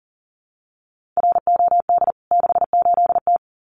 Morse code, heard as a keyed tone, about 37 wpm, 700 Hz. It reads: ROD 68T
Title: CWT Exchange 40wpm